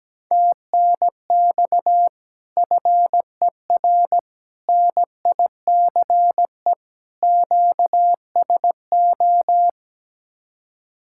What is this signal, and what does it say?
Morse code, heard as a keyed tone, about 17 wpm, 700 Hz. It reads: TNX FER NICE QSO
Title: Speed-Racing Common QSO Phrases - 17wpm